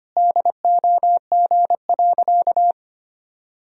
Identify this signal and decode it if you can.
Morse code, heard as a keyed tone, about 25 wpm, 700 Hz. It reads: DOG.
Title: Sentences from Top 1000 Words - Review Part 2 - 25wpm